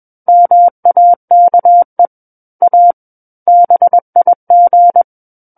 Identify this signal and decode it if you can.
Morse code, heard as a keyed tone, about 21 wpm, 700 Hz. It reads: MAKE A BIG
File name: Book One / 12-Chapter 12 - 21 WPM